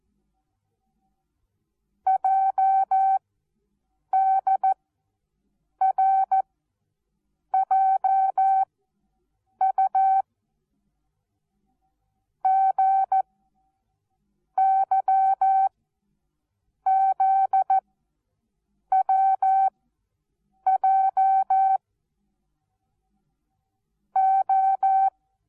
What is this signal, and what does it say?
Morse code, heard as a keyed tone, about 14 wpm, 770 Hz. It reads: J D R J U G Y Z W J O
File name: Onlinelehrgang / morsen / DJ4UF Morsen19